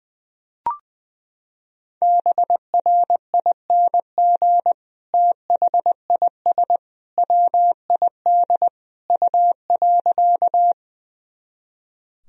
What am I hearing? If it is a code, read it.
Morse code, heard as a keyed tone, about 21 wpm, 700 Hz. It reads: BRING THIS WID U.
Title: Sentences from Top 500 Words and Abbreviations 20wpm